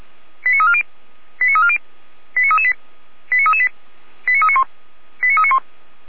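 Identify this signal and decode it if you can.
Five-tone selective call (ZVEI), heard as quick runs of five tones, tones of about 70 ms. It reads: 89340 89340 89338 89338 89391 89391
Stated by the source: ZVEI five-tone call received by SDR.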